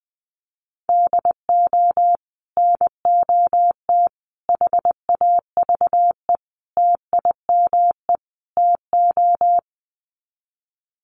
Morse code, heard as a keyed tone, about 20 wpm, 700 Hz. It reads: DO NOT HAVE TIME TO
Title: Sets of 5 Words 20wpm